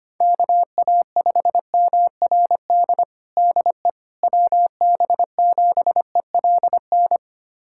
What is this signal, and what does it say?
Morse code, heard as a keyed tone, about 25 wpm, 700 Hz. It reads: KA5MRD DE WB7ELN
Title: USA Callsign DE Callsign 25wpm